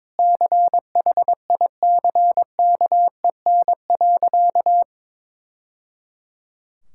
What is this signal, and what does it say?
Morse code, heard as a keyed tone, about 22 wpm, 700 Hz. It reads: CHICKEN.
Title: Sentences from Top 1000 Words - Review Part 5 - 22wpm